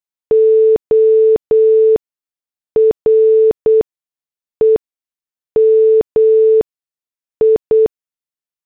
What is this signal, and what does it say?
Morse code, heard as a keyed tone, about 8 wpm, spaced as for 6 wpm, 440 Hz. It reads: OREMI